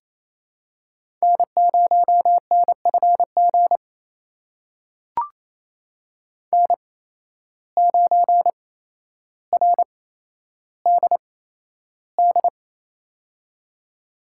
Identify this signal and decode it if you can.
Morse code, heard as a keyed tone, about 28 wpm, spaced as for 12 wpm, 700 Hz. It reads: N0NFG N9RDD
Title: USA Call Signs - 5 Characters - 8x Character Spacing - 28wpm